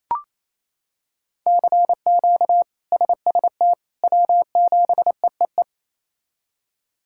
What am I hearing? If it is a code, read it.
Morse code, heard as a keyed tone, about 28 wpm, 700 Hz. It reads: CQ SST W7EEE
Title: K1USN SST QSO Practice - 28wpm